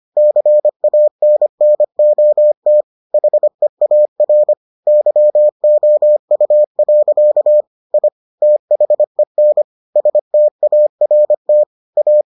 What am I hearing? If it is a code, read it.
Morse code, heard as a keyed tone, about 25 wpm, 600 Hz. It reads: CANNOT HEAR YOU. I THEN START A